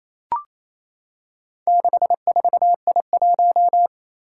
Morse code, heard as a keyed tone, about 28 wpm, 700 Hz. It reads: 64I1
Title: Four Challenging Letter-Number Combinations 28wpm